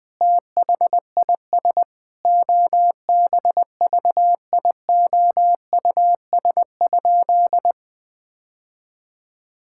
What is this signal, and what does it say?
Morse code, heard as a keyed tone, about 20 wpm, 700 Hz. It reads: THIS OBVIOUS?
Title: Sentences from Top 1600 Words 20wpm